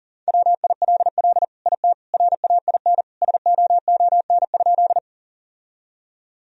Morse code, heard as a keyed tone, about 40 wpm, 700 Hz. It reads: WILL IT RAIN SOON?
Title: Sentences from Top 1000 Words - Review Part 1 - 40wpm